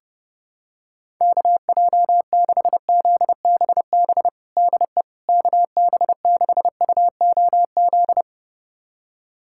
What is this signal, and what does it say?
Morse code, heard as a keyed tone, about 30 wpm, 700 Hz. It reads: KJ6ZBB DE KB6UOZ
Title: USA Callsign DE Callsign 30wpm